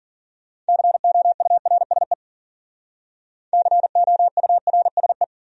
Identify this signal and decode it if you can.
Morse code, heard as a keyed tone, about 50 wpm, 700 Hz. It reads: COURSE COURSE